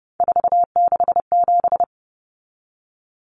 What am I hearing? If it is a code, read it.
Morse code, heard as a keyed tone, about 30 wpm, 700 Hz. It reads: 467
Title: Numbers - 3-Digit 30wpm